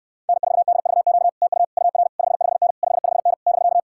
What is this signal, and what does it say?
Morse code, heard as a keyed tone, about 66 wpm, 700 Hz. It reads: N4DVQ TU UR 55N 55N <BK>